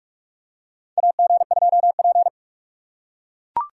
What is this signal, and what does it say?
Morse code, heard as a keyed tone, about 41 wpm, 700 Hz. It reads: AG2P
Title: USA Call Signs - 4 Characters - 2x Character Spacing - 45wpm